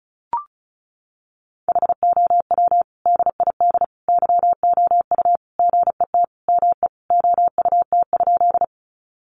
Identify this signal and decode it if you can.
Morse code, heard as a keyed tone, about 34 wpm, 700 Hz. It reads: HOW DID YOU GET ME OUT?
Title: Sentences from Top 200 Words 35wpm